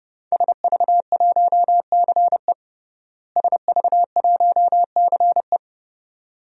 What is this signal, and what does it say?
Morse code, heard as a keyed tone, about 32 wpm, 700 Hz. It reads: SV1CE SV1CE
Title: DXCC-100 Call Signs - 5 Characters - Repeated 2x - 30wpm